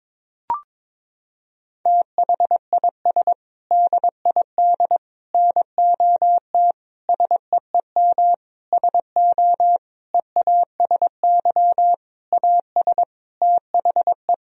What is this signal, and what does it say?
Morse code, heard as a keyed tone, about 22 wpm, 700 Hz. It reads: THIS DID NOT SEEM SO EASY AS THE
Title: Sentences from Top 1000 Words - Review Part 1 - 22wpm